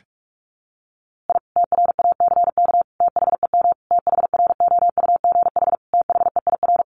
Morse code, heard as a keyed tone, about 44 wpm, 700 Hz. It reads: I TRACK THEM THROUGH THEIR
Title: A Sound Of Thunder 45wpm